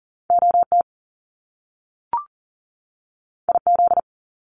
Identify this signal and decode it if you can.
Morse code, heard as a keyed tone, about 40 wpm, 700 Hz. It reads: OT IZ
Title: Top Two Letter Combinations in Words 40wpm